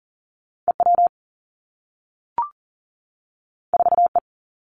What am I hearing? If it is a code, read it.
Morse code, heard as a keyed tone, about 39 wpm, 700 Hz. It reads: EW 4E
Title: Two Challenging Letter-Number Combinations 40wpm